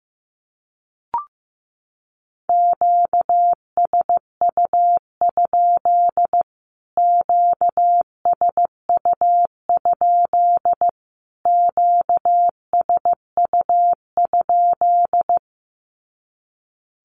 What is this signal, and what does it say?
Morse code, heard as a keyed tone, about 15 wpm, 700 Hz. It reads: QSU? QSU? QSU?
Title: Q-Codes Repeated 3x 15wpm